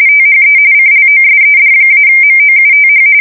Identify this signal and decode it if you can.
RTTY, about 45.45 baud. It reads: THE CLOSED-MOUTH
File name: fsk170 / usnavy4